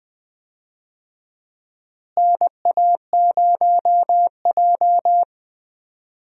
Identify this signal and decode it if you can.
Morse code, heard as a keyed tone, about 20 wpm, 700 Hz. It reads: NA0J
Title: USA Call Signs - 4 Characters - Repeated 2x - 20wpm